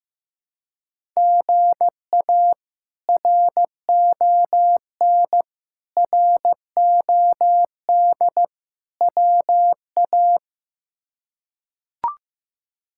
Morse code, heard as a keyed tone, about 15 wpm, 700 Hz. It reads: GA RON ROD WA
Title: K1USN SST QSO Practice - 15wpm